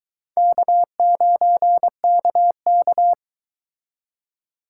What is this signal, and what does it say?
Morse code, heard as a keyed tone, about 24 wpm, 700 Hz. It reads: K9KK